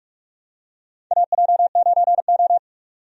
Morse code, heard as a keyed tone, about 46 wpm, 700 Hz. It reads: AJ9O
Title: USA Call Signs - 4 Characters - 45wpm